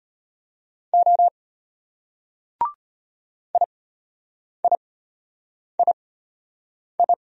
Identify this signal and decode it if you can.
Morse code, heard as a keyed tone, about 33 wpm, spaced as for 8 wpm, 700 Hz. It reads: O IIII